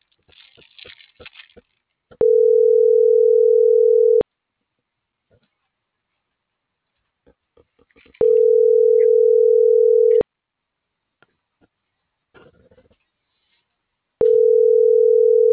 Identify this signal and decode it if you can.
Ringback tone (North America), 440 + 480 Hz.